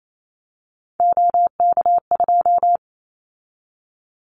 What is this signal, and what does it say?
Morse code, heard as a keyed tone, about 28 wpm, 700 Hz. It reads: OK2
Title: Callsign Prefixes for Top 100 DXCC - 28wpm